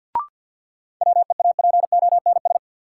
Morse code, heard as a keyed tone, about 52 wpm, 700 Hz. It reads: WEAPONS
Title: Sets of 5 Words Encore - New Word Drill - 50wpm